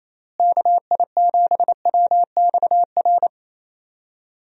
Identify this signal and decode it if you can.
Morse code, heard as a keyed tone, about 28 wpm, 700 Hz. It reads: KI7WXR
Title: USA Call Signs - 6 Characters - Repeated 3x - 28wpm